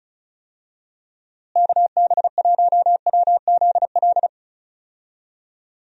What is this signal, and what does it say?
Morse code, heard as a keyed tone, about 36 wpm, 700 Hz. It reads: KB1WZL
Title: USA Call Signs - 6 Characters - Repeated 2x - 35wpm